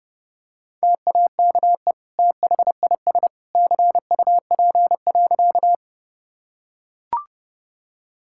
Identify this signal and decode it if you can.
Morse code, heard as a keyed tone, about 30 wpm, 700 Hz. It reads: TAKE THIS CUP.
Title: Sentences from Top 1600 Words 30wpm